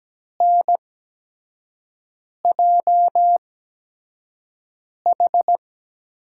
Morse code, heard as a keyed tone, about 17 wpm, spaced as for 5 wpm, 700 Hz. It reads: NJH
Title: USA Call Signs - 5 Characters - 8x Character Spacing - 17wpm